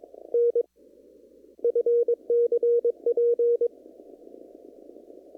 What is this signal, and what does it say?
Morse code, heard as a keyed tone, about 22 wpm, spaced as for 13 wpm, 470 Hz. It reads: NFCP
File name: CW / kt5j wpx cw-op-k5pi